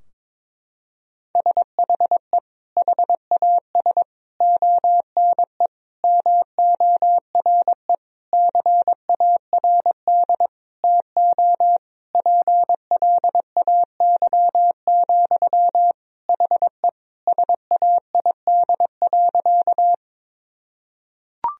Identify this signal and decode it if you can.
Morse code, heard as a keyed tone, about 22 wpm, 700 Hz. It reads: SHE HAS ONE MORE CARD TO PLAY, HE SAID.
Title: Sentences from Top 1000 Words - Review Part 4 - 22wpm